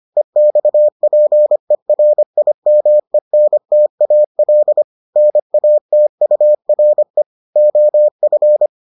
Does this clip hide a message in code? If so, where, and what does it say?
Yes: Morse code from 0.2 to 8.7 s, heard as a keyed tone, about 25 wpm, 600 Hz. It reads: EXPERIMENTAL NATURE OF